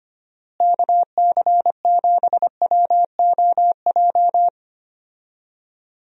Morse code, heard as a keyed tone, about 25 wpm, 700 Hz. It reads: KC7WOJ
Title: USA Call Signs - 6 Characters - Repeated 3x - 25wpm